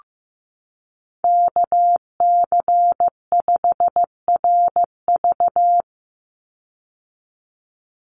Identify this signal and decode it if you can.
Morse code, heard as a keyed tone, about 15 wpm, 700 Hz. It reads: KC5RV
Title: USA Call Signs 15wpm